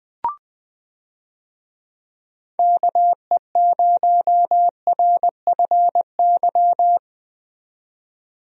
Morse code, heard as a keyed tone, about 20 wpm, 700 Hz. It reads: KE0RFY
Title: USA Call Signs - 6 Characters - 20wpm